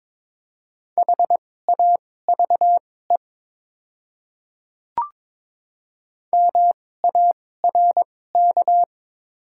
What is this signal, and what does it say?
Morse code, heard as a keyed tone, about 23 wpm, spaced as for 16 wpm, 700 Hz. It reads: HAVE MARK
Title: 2x Character Spacing - Top 500 Words - 22wpm